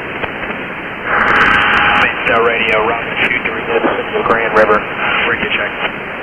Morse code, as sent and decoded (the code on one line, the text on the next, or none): none